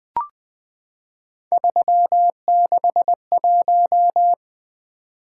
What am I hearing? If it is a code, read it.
Morse code, heard as a keyed tone, about 20 wpm, 700 Hz. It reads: 361
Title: Three Letter-Number Combinations - 20wpm